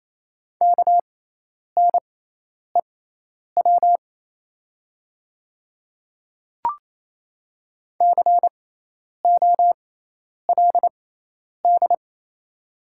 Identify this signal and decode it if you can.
Morse code, heard as a keyed tone, about 28 wpm, spaced as for 10 wpm, 700 Hz. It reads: KNEW COLD
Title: 6x Character Spacing - Top 500 Words - 28wpm